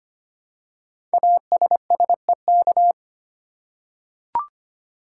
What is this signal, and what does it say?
Morse code, heard as a keyed tone, about 25 wpm, 700 Hz. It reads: ASSEK
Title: Code Groups - 5 Letters - 25wpm